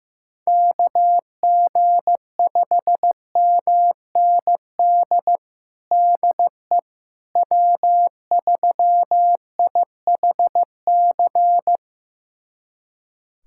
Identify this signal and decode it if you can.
Morse code, heard as a keyed tone, about 15 wpm, 700 Hz. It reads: KG5MND DE W3IHC
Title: USA Callsign DE Callsign 15wpm